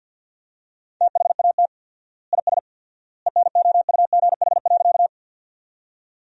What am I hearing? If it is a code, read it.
Morse code, heard as a keyed tone, about 50 wpm, 700 Hz. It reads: THAT IS ENOUGH.
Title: 4x Word-Spacing - Sentences from Top 400 Words - 50wpm